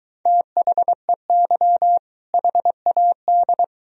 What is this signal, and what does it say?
Morse code, heard as a keyed tone, about 23 wpm, 700 Hz. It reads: THEY HAD